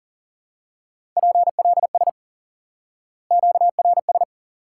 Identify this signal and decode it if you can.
Morse code, heard as a keyed tone, about 37 wpm, 700 Hz. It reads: PLS QRS
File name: IQE Course 3 - Lesson 89 - YAGI - Two Elements - 40wpm